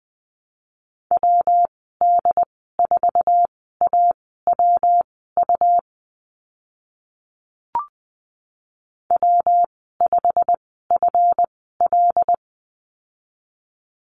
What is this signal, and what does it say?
Morse code, heard as a keyed tone, about 20 wpm, spaced as for 15 wpm, 700 Hz. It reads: WD4AWU W5FL
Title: 140 Lesson 23g - Letter f - Call Signs 20wpm